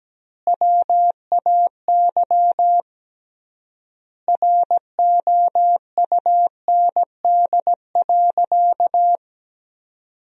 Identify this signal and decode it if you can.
Morse code, heard as a keyed tone, about 17 wpm, 700 Hz. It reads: WAY ROUND.
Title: 3x Word-Spacing - Sentences from Top 200 Words - 17wpm